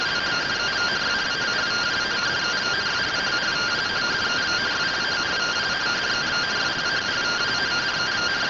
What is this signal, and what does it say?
Two-tone teleprinter signal, about 170 Hz shift.